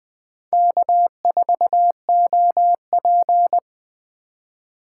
Morse code, heard as a keyed tone, about 20 wpm, 700 Hz. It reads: K4OP
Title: Speed Racing - USA Call Signs 17wpm